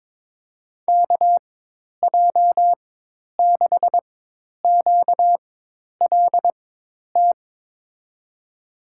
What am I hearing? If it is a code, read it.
Morse code, heard as a keyed tone, about 22 wpm, spaced as for 10 wpm, 700 Hz. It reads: KJ6QLT
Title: USA Call Signs - 6 Characters - 4x Character Spacing - 22wpm